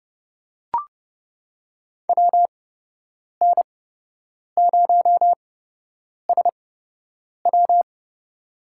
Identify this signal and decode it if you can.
Morse code, heard as a keyed tone, about 30 wpm, spaced as for 8 wpm, 700 Hz. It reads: WN0SW
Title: USA Call Signs - 5 Characters - 8x Character Spacing - 30wpm